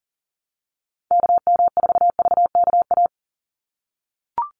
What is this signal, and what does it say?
Morse code, heard as a keyed tone, about 39 wpm, 700 Hz. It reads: KM4VKA
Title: USA Call Signs 40wpm